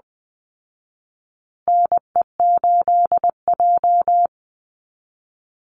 Morse code, heard as a keyed tone, about 20 wpm, 700 Hz. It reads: NE8J
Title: Call Signs Repeated 3x - 20wpm